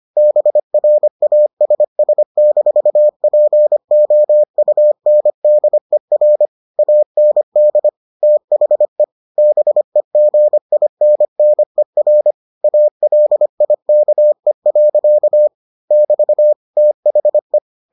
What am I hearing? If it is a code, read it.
Morse code, heard as a keyed tone, about 25 wpm, 600 Hz. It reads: BRASS-POUNDER AND THE BEGINNER ALIKE. <BT> THE